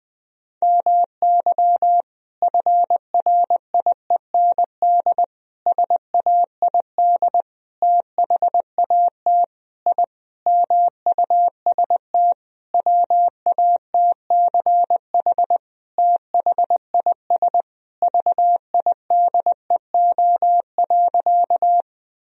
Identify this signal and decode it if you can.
Morse code, heard as a keyed tone, about 20 wpm, 700 Hz. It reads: MY FRIEND SAID THAT I MUST WATCH THIS VIDEO.
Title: Sentences from Top 1000 Words - Review Part 4 - 20wpm